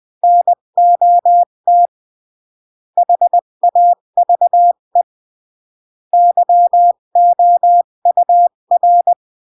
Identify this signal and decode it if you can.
Morse code, heard as a keyed tone, about 20 wpm, 700 Hz. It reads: NOT HAVE YOUR
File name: CW / jodorowsky / 82 commandments 20-15